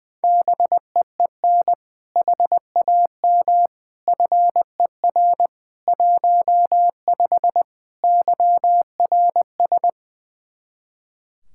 Morse code, heard as a keyed tone, about 20 wpm, 700 Hz. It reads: BEEN HAM FER 15 YRS
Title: K1USN SST QSO Practice 20wpm